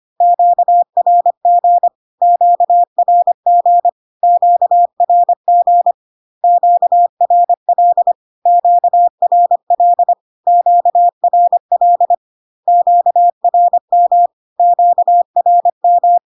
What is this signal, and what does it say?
Morse code, heard as a keyed tone, about 25 wpm, 700 Hz. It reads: QRG QRG QRG QRL QRL QRL QRM QRM
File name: Q-codes most common repeated 3 times 25 25wpm